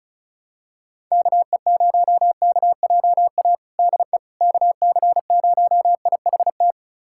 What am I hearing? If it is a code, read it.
Morse code, heard as a keyed tone, about 34 wpm, 700 Hz. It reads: KE0KJA DE KC0IHT
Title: USA Callsign DE Callsign - 6 Characters - 35wpm